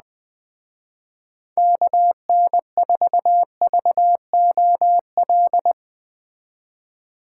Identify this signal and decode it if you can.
Morse code, heard as a keyed tone, about 20 wpm, 700 Hz. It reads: KN4VOL